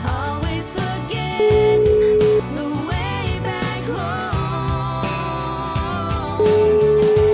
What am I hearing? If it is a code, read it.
Ringback tone (Europe, 425 Hz), with noise.